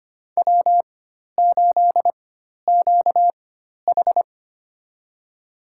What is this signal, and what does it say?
Morse code, heard as a keyed tone, about 25 wpm, spaced as for 12 wpm, 700 Hz. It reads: W8QH
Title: USA Call Signs - 4 Characters - 4x Character Spacing - 25wpm